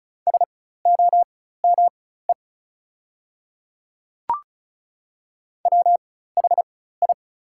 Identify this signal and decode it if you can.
Morse code, heard as a keyed tone, about 34 wpm, spaced as for 16 wpm, 700 Hz. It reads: SOME WHI
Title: 4x Character Spacing - Top 100 Words - 35wpm